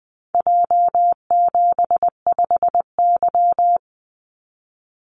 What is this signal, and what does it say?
Morse code, heard as a keyed tone, about 20 wpm, 700 Hz. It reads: J75Y
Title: Call Signs Repeated 3x - 20wpm